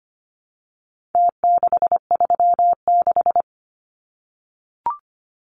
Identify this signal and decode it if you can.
Morse code, heard as a keyed tone, about 25 wpm, 700 Hz. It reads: T636
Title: Numbers - Contest Style - 4-Digit 25wpm